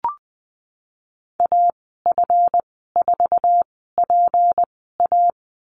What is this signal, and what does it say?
Morse code, heard as a keyed tone, about 19 wpm, spaced as for 14 wpm, 700 Hz. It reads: AF4PA
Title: 179 Lesson 29g - Slash - Call Signs 20wpm